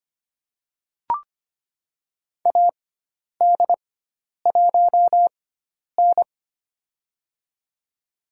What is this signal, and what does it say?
Morse code, heard as a keyed tone, about 25 wpm, spaced as for 10 wpm, 700 Hz. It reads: AD1N
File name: course / 25:10wpm / 095 Lesson 16g - Letter c - Call Signs 25wpm